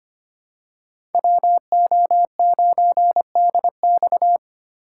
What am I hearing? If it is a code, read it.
Morse code, heard as a keyed tone, about 25 wpm, 700 Hz. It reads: WO9DX
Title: USA Call Signs - 5 Characters - 2x Character Spacing - 25wpm